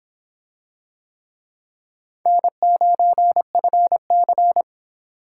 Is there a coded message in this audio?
Yes: Morse code from 2.3 to 4.6 s, heard as a keyed tone, about 25 wpm, 700 Hz. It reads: N9FC